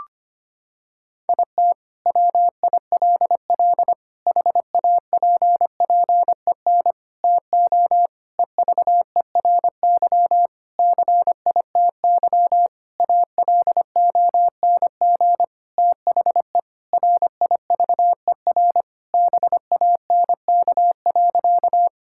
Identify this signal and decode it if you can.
Morse code, heard as a keyed tone, about 25 wpm, 700 Hz. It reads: IT WILL HAPPEN TO EVERY CITY ALONG THE RIVER BANK.
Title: Sentences from Top 1000 Words - Review Part 6 - 25wpm